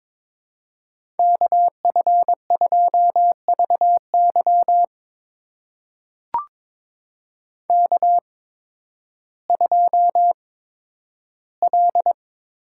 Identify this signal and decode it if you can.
Morse code, heard as a keyed tone, about 22 wpm, spaced as for 12 wpm, 700 Hz. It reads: KF2VY K2L
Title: USA Call Signs - 8x Character Spacing - 22wpm